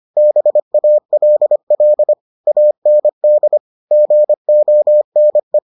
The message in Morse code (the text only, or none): BALL AND GONE